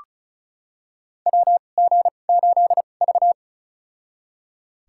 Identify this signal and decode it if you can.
Morse code, heard as a keyed tone, about 36 wpm, spaced as for 25 wpm, 700 Hz. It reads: WG8V
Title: USA Call Signs - 4 Characters - 2x Character Spacing - 35wpm